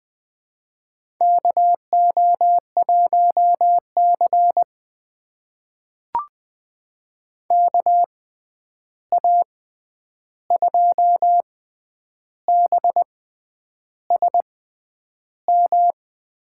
Morse code, heard as a keyed tone, about 21 wpm, spaced as for 9 wpm, 700 Hz. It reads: KO1C KA2BSM
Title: USA Call Signs - 6x Character Spacing - 20wpm